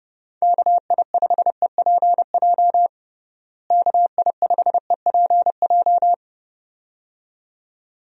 Morse code, heard as a keyed tone, about 30 wpm, 700 Hz. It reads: KI5EPJ KI5EPJ
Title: USA Call Signs - 6 Characters - Repeated 2x - 30wpm